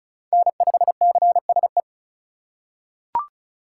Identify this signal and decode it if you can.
Morse code, heard as a keyed tone, about 36 wpm, 700 Hz. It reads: N5CSE
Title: USA Call Signs - 5 Characters - 3x Character Spacing - 35wpm